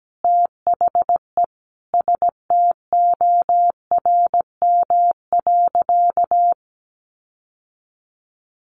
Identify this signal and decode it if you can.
Morse code, heard as a keyed tone, about 17 wpm, 700 Hz. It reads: THE STORM.
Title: Sentences from Top 1500 Words 17wpm